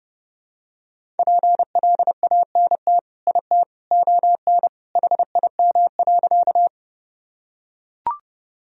Morse code, heard as a keyed tone, about 30 wpm, 700 Hz. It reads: PLANT IT ON HIM.
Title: Sentences from Top 1000 Words - Review Part 1 - 30wpm